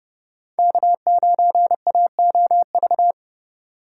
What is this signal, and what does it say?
Morse code, heard as a keyed tone, about 32 wpm, 700 Hz. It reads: K9AOV
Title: USA Call Signs - 4x Character Spacing - 30wpm